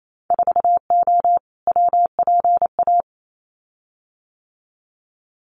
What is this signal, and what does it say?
Morse code, heard as a keyed tone, about 28 wpm, 700 Hz. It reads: 4O WPA
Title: Winter Field Day Exchanges 28wpm